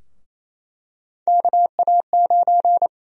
Morse code, heard as a keyed tone, about 28 wpm, 700 Hz. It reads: KA9
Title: USA Call Sign Prefixes - Repeated 2x - 28wpm